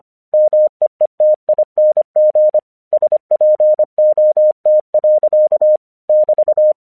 Morse code, heard as a keyed tone, about 24 wpm, 610 Hz. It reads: MEETING SPOT. <BT>
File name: 1 English 25wpm / AQC07